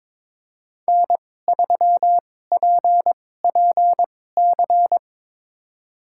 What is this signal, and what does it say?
Morse code, heard as a keyed tone, about 22 wpm, spaced as for 16 wpm, 700 Hz. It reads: N3PPC
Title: USA Call Signs - 5 Characters - 2x Character Spacing - 22wpm